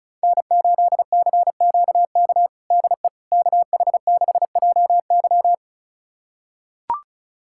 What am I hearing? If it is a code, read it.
Morse code, heard as a keyed tone, about 34 wpm, 700 Hz. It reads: N8CQK DE KH6JY
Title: USA Callsign DE Callsign - 5 Characters - 35wpm